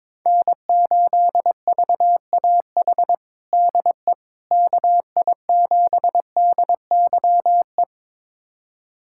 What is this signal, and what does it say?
Morse code, heard as a keyed tone, about 22 wpm, 700 Hz. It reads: N8VAH DE KI7DYE
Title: USA Callsign DE Callsign 22wpm